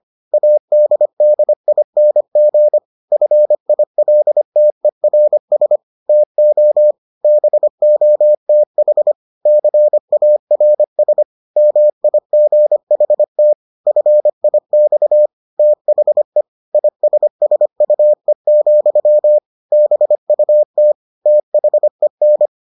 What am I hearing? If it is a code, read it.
Morse code, heard as a keyed tone, about 25 wpm, 600 Hz. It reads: ADDING FILTERS TO BOTH CARS MIGHT FIX THE ISSUE, BUT THEN